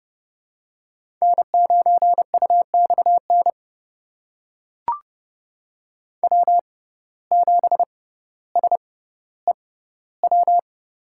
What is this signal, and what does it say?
Morse code, heard as a keyed tone, about 31 wpm, spaced as for 16 wpm, 700 Hz. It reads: N9UXN W7SEW
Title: USA Call Signs - 5 Characters - 6x Character Spacing - 30wpm